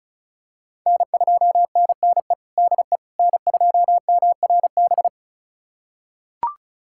Morse code, heard as a keyed tone, about 36 wpm, 700 Hz. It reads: N2NNE DE N2MRB
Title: USA Callsign DE Callsign - 5 Characters - 35wpm